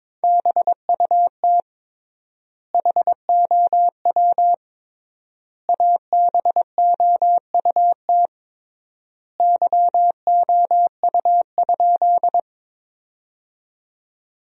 Morse code, heard as a keyed tone, about 22 wpm, 700 Hz. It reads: BUT HOW ABOUT YOU?
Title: 3x Word-Spacing - Sentences from Top 100 Words - 22wpm